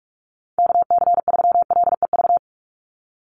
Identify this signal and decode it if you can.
Morse code, heard as a keyed tone, about 44 wpm, 700 Hz. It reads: KC3LEV
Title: USA Call Signs 45wpm